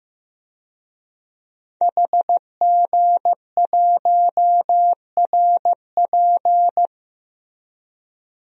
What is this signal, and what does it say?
Morse code, heard as a keyed tone, about 16 wpm, 700 Hz. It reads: HG1RP
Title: DXCC-100 Call Signs - 5 Characters - Repeated 3x - 15wpm